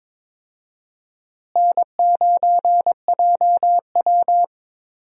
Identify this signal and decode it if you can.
Morse code, heard as a keyed tone, about 22 wpm, 700 Hz. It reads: N9JW
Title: USA Call Signs - 4 Characters - Repeated 3x - 22wpm